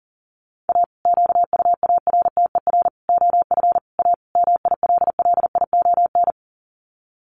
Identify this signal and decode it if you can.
Morse code, heard as a keyed tone, about 40 wpm, 700 Hz. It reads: A QUARTER OF A MILLION
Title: Sets of 5 Words Encore2 40wpm